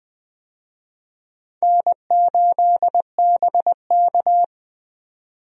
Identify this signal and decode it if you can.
Morse code, heard as a keyed tone, about 20 wpm, 700 Hz. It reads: N8BK
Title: USA Call Signs - 4 Characters - 20wpm